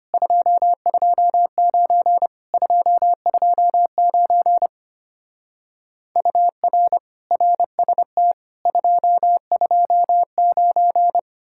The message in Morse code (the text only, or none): 229 229 UR RST 229